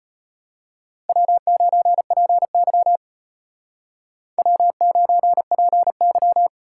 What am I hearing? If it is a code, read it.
Morse code, heard as a keyed tone, about 32 wpm, 700 Hz. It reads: W9PY W9PY